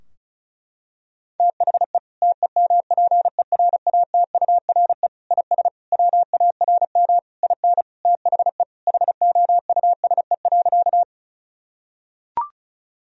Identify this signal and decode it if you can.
Morse code, heard as a keyed tone, about 34 wpm, 700 Hz. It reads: THE TEMPERATURE IS WARM IN THE HOUSE.
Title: Sentences from Top 1000 Words - Review Part 4 - 35wpm